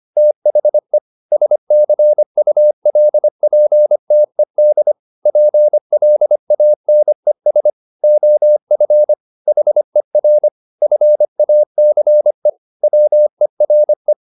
Morse code, heard as a keyed tone, about 25 wpm, 600 Hz. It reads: THE SCULPTED PLANES OF HER FACE WERE